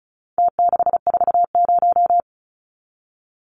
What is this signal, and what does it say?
Morse code, heard as a keyed tone, about 34 wpm, 700 Hz. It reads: T640
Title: Numbers - Contest Style - 4-Digit 35wpm